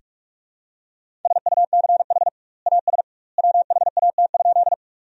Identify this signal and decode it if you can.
Morse code, heard as a keyed tone, about 46 wpm, 700 Hz. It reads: SUCH AS WHAT?
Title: 2x Word-Spacing - Sentences from Top 200 Words - 45wpm